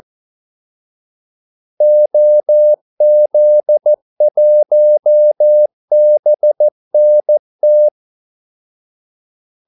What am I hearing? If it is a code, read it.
Morse code, heard as a keyed tone, about 14 wpm, 600 Hz. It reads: OZ1BNT
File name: Callsigns-14wpm